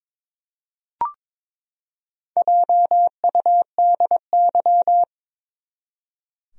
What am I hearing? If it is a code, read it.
Morse code, heard as a keyed tone, about 22 wpm, 700 Hz. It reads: JUDY
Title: USA Common Names - Max 5 Letters - 22wpm